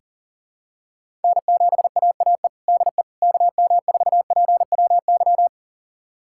Morse code, heard as a keyed tone, about 40 wpm, 700 Hz. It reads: N7AAE DE KM4PWY
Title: USA Callsign DE Callsign 40wpm